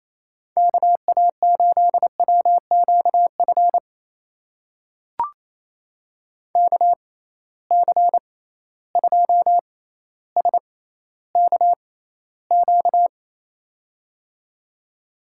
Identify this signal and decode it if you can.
Morse code, heard as a keyed tone, about 28 wpm, spaced as for 14 wpm, 700 Hz. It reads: KA8WQF KC2SKQ
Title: USA Call Signs - 6 Characters - 6x Character Spacing - 28wpm